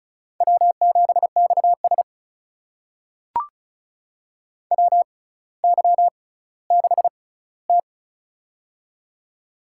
Morse code, heard as a keyed tone, about 36 wpm, spaced as for 18 wpm, 700 Hz. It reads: W7XS WY6T
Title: USA Call Signs - 4 Characters - 6x Character Spacing - 35wpm